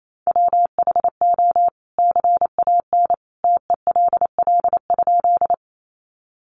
Morse code, heard as a keyed tone, about 28 wpm, 700 Hz. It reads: WHO CAN TELL?
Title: A Sound Of Thunder 28wpm